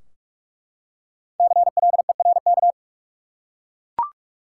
Morse code, heard as a keyed tone, about 44 wpm, 700 Hz. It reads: CLERK
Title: Five Letter Words - 45wpm